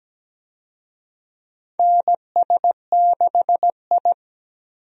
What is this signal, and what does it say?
Morse code, heard as a keyed tone, about 17 wpm, 700 Hz. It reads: NS6I
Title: USA Call Signs - 4 Characters - 17wpm